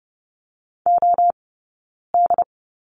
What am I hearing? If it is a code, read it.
Morse code, heard as a keyed tone, about 30 wpm, 700 Hz. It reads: O D
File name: Lesson 20d - Number 3 - Two Characters 30wpm